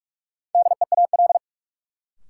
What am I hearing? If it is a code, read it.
Morse code, heard as a keyed tone, about 46 wpm, 700 Hz. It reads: DEAL
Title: Sentences from Top 800 Words - New Word Drill - 45wpm